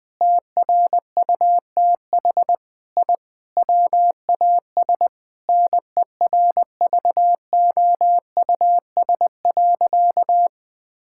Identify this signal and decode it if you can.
Morse code, heard as a keyed tone, about 20 wpm, 700 Hz. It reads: TRUTH I WAS NERVOUS.
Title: Sentences with 5-Word Phrases - Spoken Beginning - 20wpm